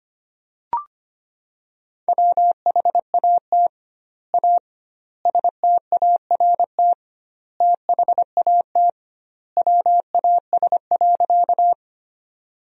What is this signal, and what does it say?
Morse code, heard as a keyed tone, about 25 wpm, 700 Hz. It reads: WHAT A START THAT WAS.
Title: 2x Word-Spacing - Sentences from Top 300 Words - 25wpm